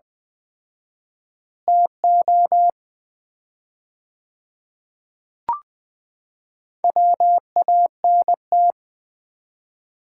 Morse code, heard as a keyed tone, about 20 wpm, 700 Hz. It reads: TO WANT
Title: 4x Word-Spacing - Sets of 4 Words - 20wpm